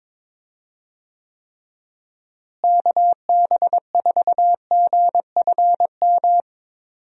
Morse code, heard as a keyed tone, about 22 wpm, 700 Hz. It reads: KB4GFM
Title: USA Call Signs - 6 Characters - Repeated 2x - 22wpm